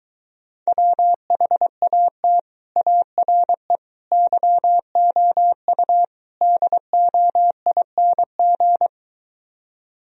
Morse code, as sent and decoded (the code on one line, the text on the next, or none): .-- .... .- - / .- .-. / -.-- --- ..- / -.. --- .. -. --.
WHAT ARE YOU DOING